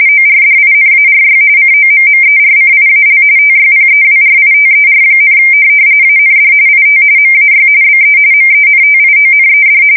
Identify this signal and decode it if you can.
RTTY, about 45.45 baud. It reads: ORDINARY COMMERCIAL TYPEWRITER, AND IT MAY BE MENTIONED
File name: fsk170 / morkrum